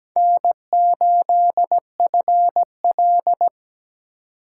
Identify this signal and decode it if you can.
Morse code, heard as a keyed tone, about 20 wpm, 700 Hz. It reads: N8FL
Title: Speed Racing - USA Call Signs 17wpm